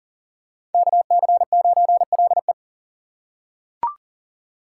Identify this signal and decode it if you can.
Morse code, heard as a keyed tone, about 40 wpm, 700 Hz. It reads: KC9LE
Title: USA Call Signs - 5 Characters - 2x Character Spacing - 40wpm